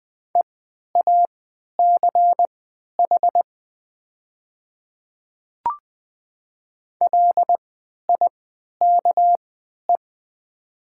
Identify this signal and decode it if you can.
Morse code, heard as a keyed tone, about 20 wpm, spaced as for 11 wpm, 700 Hz. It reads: EACH LIKE
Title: 3x Character Spacing - Top 200 Words - 20wpm